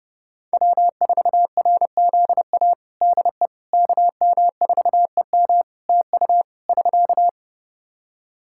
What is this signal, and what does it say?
Morse code, heard as a keyed tone, about 31 wpm, 700 Hz. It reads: W4RZA DE KM4EM TU <SK>
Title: Practice QSO - Exchange 4 - Ending - 30wpm